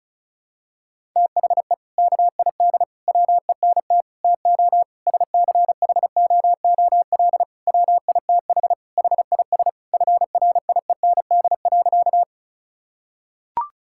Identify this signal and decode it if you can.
Morse code, heard as a keyed tone, about 36 wpm, 700 Hz. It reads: THE KID WENT TO SCHOOL WITH HIS FRIEND.
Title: Sentences from Top 1600 Words 35wpm